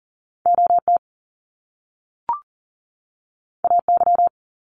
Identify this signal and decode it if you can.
Morse code, heard as a keyed tone, about 39 wpm, 700 Hz. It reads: OT AY
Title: Top Two Letter Combinations in Words 40wpm